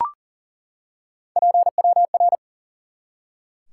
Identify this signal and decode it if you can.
Morse code, heard as a keyed tone, about 40 wpm, 700 Hz. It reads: PWR
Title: Top 90 QSO Elements 40wpm